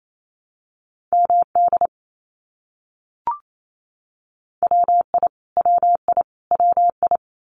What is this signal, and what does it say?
Morse code, heard as a keyed tone, about 28 wpm, 700 Hz. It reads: MD WI WI WI
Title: US States Repeated 3x 28wpm